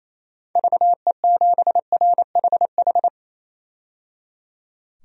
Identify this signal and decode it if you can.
Morse code, heard as a keyed tone, about 28 wpm, 700 Hz. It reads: VE7RHH
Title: DXCC-100 Call Signs - 6 Characters - 28wpm